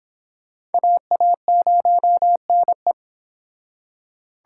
Morse code, heard as a keyed tone, about 26 wpm, 700 Hz. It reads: AA0NE
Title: Speed Racing - USA 2x2 Call Signs 22wpm